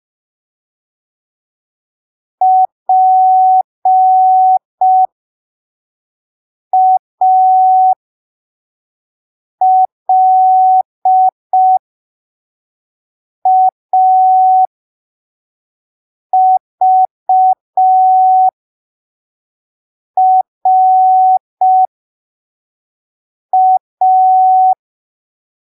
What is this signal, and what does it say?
Morse code, heard as a keyed tone, about 5 wpm, spaced as for 3 wpm, 750 Hz. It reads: PALAVRA
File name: arquivos / texto / texto 25 05ppm